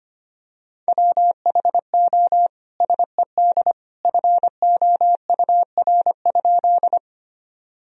Morse code, heard as a keyed tone, about 25 wpm, 700 Hz. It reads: WHO SED FOUR?
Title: Sentences from Top 300 Words and Abbreviations 25wpm